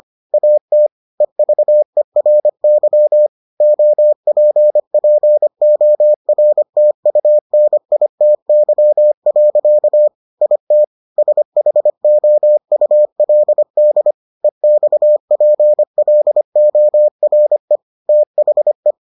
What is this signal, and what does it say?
Morse code, heard as a keyed tone, about 25 wpm, 600 Hz. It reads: AT EVERY OPPORTUNITY. IT SHOULD EXPLORE THE